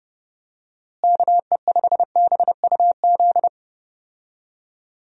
Morse code, heard as a keyed tone, about 32 wpm, 700 Hz. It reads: KE5BUZ
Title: USA Call Signs - 6 Characters - Repeated 2x - 30wpm